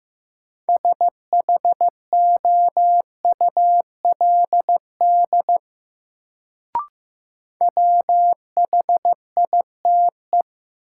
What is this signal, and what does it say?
Morse code, heard as a keyed tone, about 15 wpm, 700 Hz. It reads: SHOULD WHITE
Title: Sets of 2 Words - New Word Drill - 15wpm